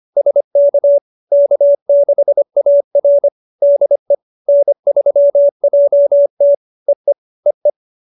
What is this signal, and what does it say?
Morse code, heard as a keyed tone, about 25 wpm, 570 Hz. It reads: SK K6AR DE N3JT EE EE